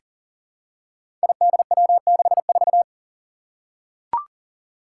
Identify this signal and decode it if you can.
Morse code, heard as a keyed tone, about 40 wpm, 700 Hz. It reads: IDW64
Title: Five Challenging Letter-Number Combinations 40wpm